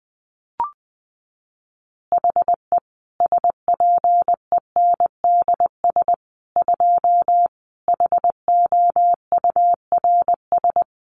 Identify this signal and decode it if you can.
Morse code, heard as a keyed tone, about 20 wpm, 700 Hz. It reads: HE SPENDS 2 HOURS
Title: Additional Easy Phrases 20wpm